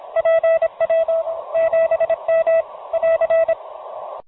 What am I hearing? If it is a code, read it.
Morse code, heard as a keyed tone, about 26 wpm, 660 Hz, with radio noise. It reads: PW 7M <AR>
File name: 04-07-05-pj7m-beacon